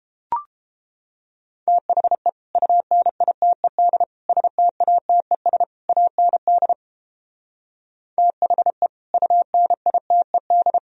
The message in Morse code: THE UNITED STATES AND THE UNITED

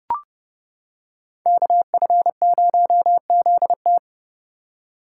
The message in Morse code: KF0ZT

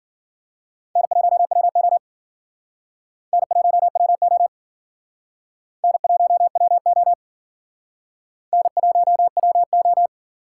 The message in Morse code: N1WO N1WO N1WO N1WO

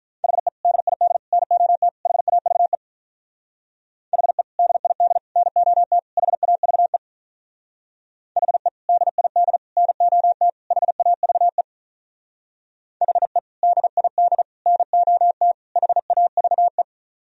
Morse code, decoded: HE DID NOT HAVE HE DID NOT HAVE HE DID NOT HAVE HE DID NOT HAVE

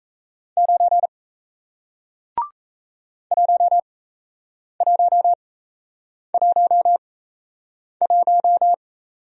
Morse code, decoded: 9 1111

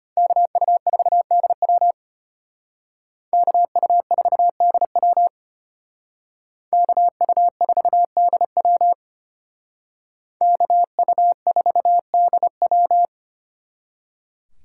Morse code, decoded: KU4DW KU4DW KU4DW KU4DW

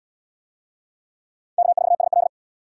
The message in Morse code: K4SL

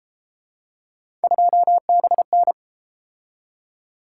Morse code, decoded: 2BN

